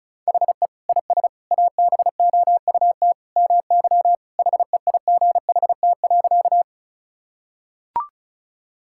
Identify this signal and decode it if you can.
Morse code, heard as a keyed tone, about 36 wpm, 700 Hz. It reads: HE IS ABOUT MY HEIGHT.
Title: Sentences from Top 1600 Words 35wpm